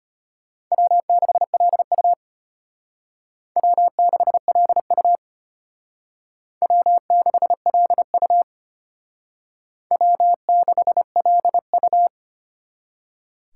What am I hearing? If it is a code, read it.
Morse code, heard as a keyed tone, about 31 wpm, 700 Hz. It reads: W6LU W6LU W6LU W6LU